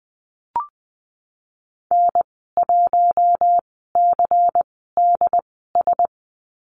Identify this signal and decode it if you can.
Morse code, heard as a keyed tone, about 20 wpm, spaced as for 15 wpm, 700 Hz. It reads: N1CDS
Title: 200 Lesson 32g - Letter k - Call Signs 20wpm